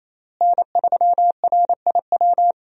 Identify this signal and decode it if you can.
Morse code, heard as a keyed tone, about 28 wpm, 700 Hz. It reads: N3RIW
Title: USA Call Signs - 5 Characters - Repeated 3x - 28wpm